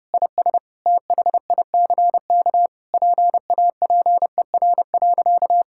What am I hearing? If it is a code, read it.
Morse code, heard as a keyed tone, about 30 wpm, 700 Hz. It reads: IS THICK PAPER.
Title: Sentences from Top 1600 Words 30wpm